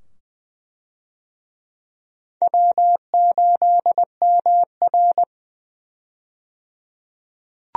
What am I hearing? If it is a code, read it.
Morse code, heard as a keyed tone, about 20 wpm, 700 Hz. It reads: W8MR
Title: USA Call Signs - 4 Characters - 20wpm